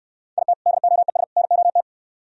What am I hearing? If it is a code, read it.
Morse code, heard as a keyed tone, about 64 wpm, 700 Hz. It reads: IT DOES NOT